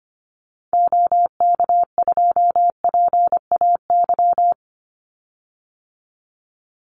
Morse code, heard as a keyed tone, about 25 wpm, 700 Hz. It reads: OK2PAY
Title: Call Signs - 25wpm